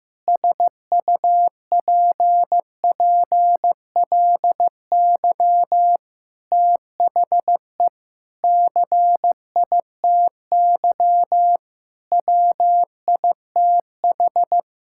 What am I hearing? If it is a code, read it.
Morse code, heard as a keyed tone, about 15 wpm, 700 Hz. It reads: SUPPLY THE CITY WITH
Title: Sentences from Top 1000 Words - Review Part 5 - 15wpm